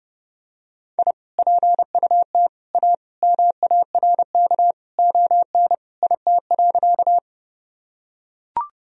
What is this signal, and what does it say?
Morse code, heard as a keyed tone, about 30 wpm, 700 Hz. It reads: I PUT A MARK ON IT.
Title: Sentences from Top 1000 Words - Review Part 1 - 30wpm